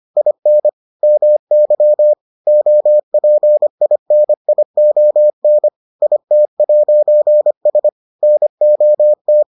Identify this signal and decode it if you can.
Morse code, heard as a keyed tone, about 25 wpm, 600 Hz. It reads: IN MY OPINION IT'S NOT